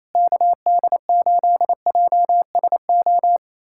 Morse code, heard as a keyed tone, about 28 wpm, 700 Hz. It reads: KD8JSO
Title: USA Callsign DE Callsign 28wpm